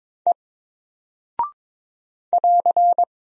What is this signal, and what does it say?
Morse code, heard as a keyed tone, about 22 wpm, 700 Hz. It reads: E <AR>
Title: Single Character (Letter:Number:Punctuation:Prosign) Alt 22wpm